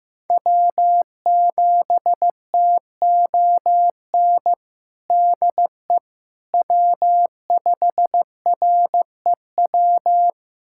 Morse code, heard as a keyed tone, about 15 wpm, 700 Hz. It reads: W7TON DE W5REW
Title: USA Callsign DE Callsign - 5 Characters - 15wpm